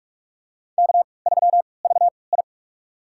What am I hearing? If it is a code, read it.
Morse code, heard as a keyed tone, about 44 wpm, spaced as for 25 wpm, 690 Hz. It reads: K3VI
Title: USA Call Signs - 4 Characters - 3x Character Spacing - 45wpm